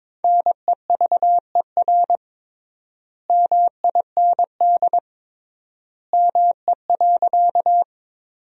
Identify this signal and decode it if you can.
Morse code, heard as a keyed tone, about 22 wpm, 700 Hz. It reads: NEVER MIND ME.
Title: 3x Word-Spacing - Sentences from Top 500 Words - 22wpm